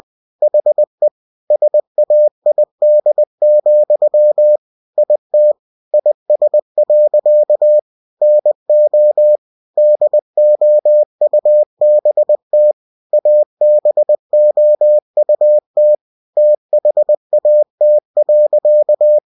Morse code, heard as a keyed tone, about 20 wpm, 600 Hz. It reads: HE SAID, IT IS. NO DOUBT ABOUT THAT.
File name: cw-books / winnie-the-pooh / winnie-20wpm-page-0037